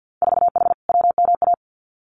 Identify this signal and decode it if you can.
Morse code, heard as a keyed tone, about 50 wpm, 700 Hz. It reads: 4H WMA
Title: Winter Field Day Exchanges Encore 50wpm